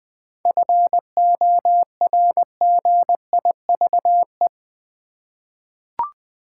Morse code, heard as a keyed tone, about 21 wpm, 700 Hz. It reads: FORGIVE
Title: 7 Letter Words - First 3 Letters Given - 20wpm